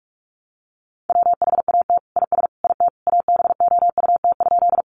Morse code, heard as a keyed tone, about 44 wpm, 700 Hz. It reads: WHAT IS IT ABOUT?
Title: Sentences from Top 100 Words 45wpm